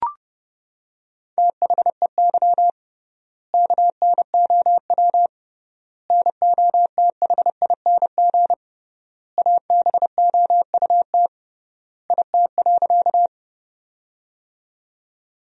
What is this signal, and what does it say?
Morse code, heard as a keyed tone, about 28 wpm, 700 Hz. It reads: THEY KNOW NOTHING ABOUT IT.
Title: 3x Word-Spacing - Sentences from Top 500 Words - 30wpm